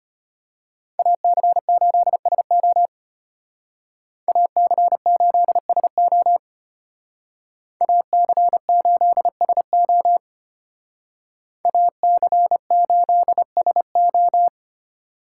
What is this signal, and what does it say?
Morse code, heard as a keyed tone, about 33 wpm, 700 Hz. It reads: AC8SO AC8SO AC8SO AC8SO